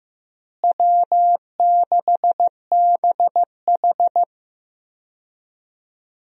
Morse code, heard as a keyed tone, about 15 wpm, 700 Hz. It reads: W6BH
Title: USA Call Signs - 4 Characters - Repeated 2x - 15wpm